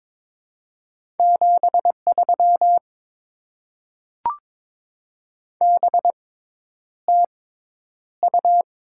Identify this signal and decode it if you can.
Morse code, heard as a keyed tone, about 22 wpm, spaced as for 10 wpm, 700 Hz. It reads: 73 BTU
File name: Top 20 QSO Elements - 6x Character Spacing - 22wpm